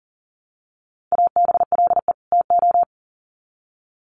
Morse code, heard as a keyed tone, about 40 wpm, 700 Hz. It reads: ABLE TO
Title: Sets of 2 Words 40wpm